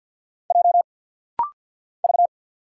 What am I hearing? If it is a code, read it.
Morse code, heard as a keyed tone, about 48 wpm, 700 Hz. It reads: J V
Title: Single Letter-Number - Rapid-Fire 50wpm